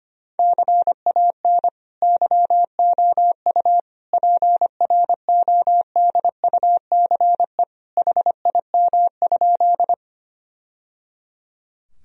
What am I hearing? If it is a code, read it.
Morse code, heard as a keyed tone, about 25 wpm, 700 Hz. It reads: CAN YOU PRODUCE HIM?
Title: Sentences from Top 1000 Words - Review Part 3 - 25wpm